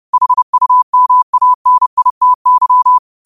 Morse code, heard as a keyed tone, about 30 wpm, 1000 Hz. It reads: HUMANITY
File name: qotdfiles / 30wpm